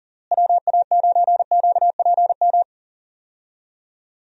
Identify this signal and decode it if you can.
Morse code, heard as a keyed tone, about 40 wpm, 700 Hz. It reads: WA9QPM
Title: USA Call Signs - 6 Characters - Repeated 3x - 40wpm